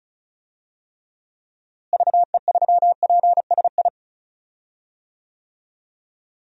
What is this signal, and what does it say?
Morse code, heard as a keyed tone, about 34 wpm, 700 Hz. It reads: VE3PSI
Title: DXCC-100 Call Signs - 6 Characters - Repeated 2x - 35wpm